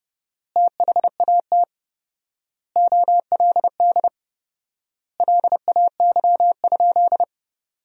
Morse code, heard as a keyed tone, about 30 wpm, 700 Hz. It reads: THAT OLD LAY?
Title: 4x Word-Spacing - Sentences from Top 400 Words - 30wpm